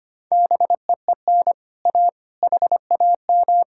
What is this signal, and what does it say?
Morse code, heard as a keyed tone, about 31 wpm, 700 Hz. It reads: BEEN A HAM
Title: Speed-Racing Common QSO Phrases - 25wpm